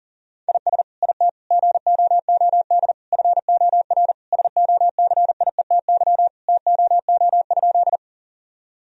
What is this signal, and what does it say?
Morse code, heard as a keyed tone, about 37 wpm, 700 Hz. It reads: IS IT GOOD FOR SOCIETY TOO?
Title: Sentences from Top 1000 Words - Review Part 4 - 40wpm